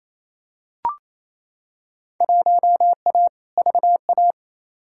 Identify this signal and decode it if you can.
Morse code, heard as a keyed tone, about 28 wpm, 700 Hz. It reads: 1A VA
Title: Field Day Exchanges Encore - 28wpm